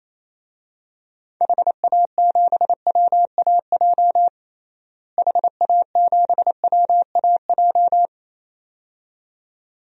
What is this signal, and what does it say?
Morse code, heard as a keyed tone, about 28 wpm, 700 Hz. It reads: HA7WAJ HA7WAJ
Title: DXCC-100 Call Signs - 6 Characters - Repeated 2x - 28wpm